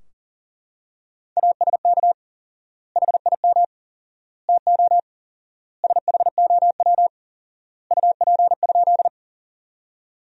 Morse code, heard as a keyed tone, about 40 wpm, 700 Hz. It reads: ASK HIM TO SHOW UP?
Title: 4x Word-Spacing - Sentences from Top 200 Words - 40wpm